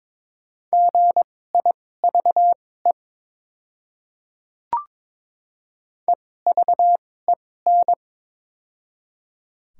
Morse code, heard as a keyed tone, about 21 wpm, spaced as for 16 wpm, 700 Hz. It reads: GIVE EVEN
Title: 2x Character Spacing - Top 200 Words - 22wpm